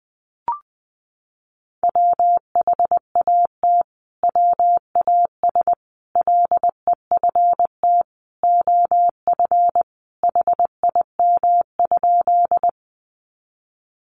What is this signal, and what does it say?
Morse code, heard as a keyed tone, about 20 wpm, 700 Hz. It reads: WHAT WAS LEFT OF HIM?
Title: Sentences from Top 300 Words 20wpm